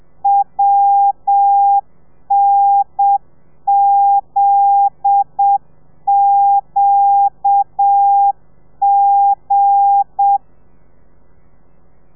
Morse code, heard as a keyed tone, about 7 wpm, 800 Hz. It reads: WNZQG